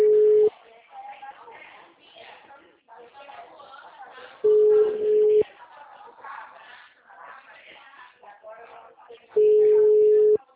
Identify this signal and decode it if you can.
Ringback tone (Europe, 425 Hz), with speech.